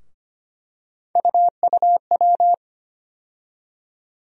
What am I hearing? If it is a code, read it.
Morse code, heard as a keyed tone, about 25 wpm, 700 Hz. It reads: UUW
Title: USA Call Sign Postfixes - 3 Letters 25wpm